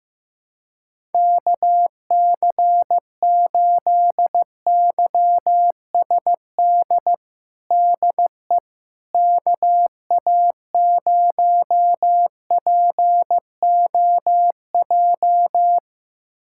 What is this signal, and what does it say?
Morse code, heard as a keyed tone, about 16 wpm, 700 Hz. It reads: KC8YSD DE KA0POJ
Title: USA Callsign DE Callsign - 6 Characters - 15wpm